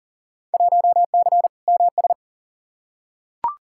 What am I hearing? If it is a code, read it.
Morse code, heard as a keyed tone, about 40 wpm, 700 Hz. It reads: 1C MS
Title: Field Day Exchanges Encore - 40wpm